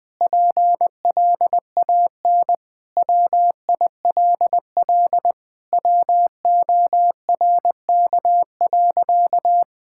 Morse code, heard as a keyed tone, about 20 wpm, 700 Hz. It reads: PLAN WILL WORK.
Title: Sentences with 7-Word Phrases - 20wpm